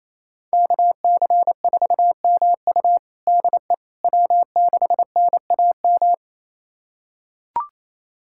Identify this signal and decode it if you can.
Morse code, heard as a keyed tone, about 28 wpm, 700 Hz. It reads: KC4MU DE W6NAM
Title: USA Callsign DE Callsign - 5 Characters - 28wpm